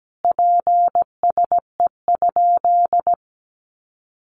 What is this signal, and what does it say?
Morse code, heard as a keyed tone, about 17 wpm, 700 Hz. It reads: PSE?
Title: Common QSO Phrases 17wpm